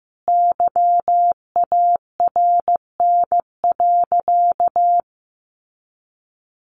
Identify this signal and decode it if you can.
Morse code, heard as a keyed tone, about 15 wpm, 700 Hz. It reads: YARN.
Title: Easy Phrases 15wpm